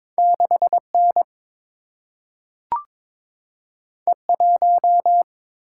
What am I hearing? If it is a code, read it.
Morse code, heard as a keyed tone, about 22 wpm, 700 Hz. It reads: 6 N E 1
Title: Single-Letter Single-Number 22wpm